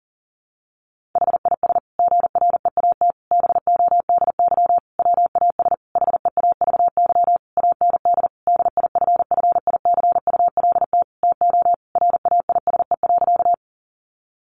Morse code, heard as a keyed tone, about 40 wpm, 700 Hz. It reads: HIS GREAT BODY WAS HEAVY AND DIFFICULT TO RAISE.
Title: Sentences from Top 1400 Words 40wpm